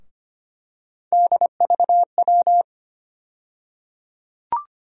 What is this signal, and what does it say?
Morse code, heard as a keyed tone, about 25 wpm, 700 Hz. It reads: DVW
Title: USA Call Sign Postfixes - 3 Letters 25wpm